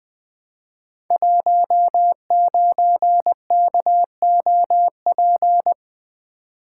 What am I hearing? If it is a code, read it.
Morse code, heard as a keyed tone, about 20 wpm, 700 Hz. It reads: 19KOP
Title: Code Groups - 5 Letter:Numbers - 20wpm